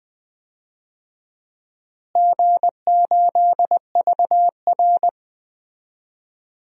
Morse code, heard as a keyed tone, about 20 wpm, 700 Hz. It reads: G8VR
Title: DXCC-100 Call Signs - 4 Characters - Repeated 2x - 20wpm